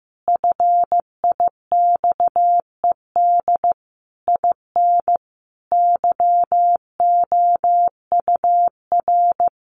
Morse code, heard as a keyed tone, about 14 wpm, 700 Hz. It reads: FIXED IN YOUR
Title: Sentences from Top 1100 Words 15wpm